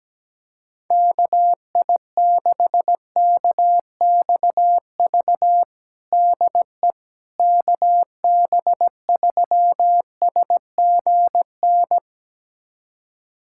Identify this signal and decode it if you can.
Morse code, heard as a keyed tone, about 17 wpm, 700 Hz. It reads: KI6KXV DE KB3SGN
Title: USA Callsign DE Callsign 17wpm